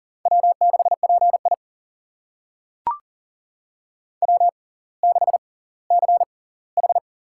Morse code, heard as a keyed tone, about 40 wpm, spaced as for 20 wpm, 700 Hz. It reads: W6PI W6CH
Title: USA Call Signs - 4 Characters - 6x Character Spacing - 40wpm